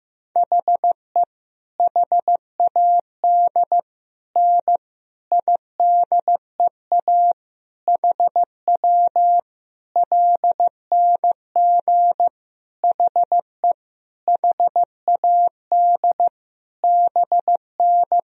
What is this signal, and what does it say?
Morse code, heard as a keyed tone, about 15 wpm, 700 Hz. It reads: HE HAD N IDEA HW LNG HE HAD BN
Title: Sentences from Top 300 Words and Abbreviations 15wpm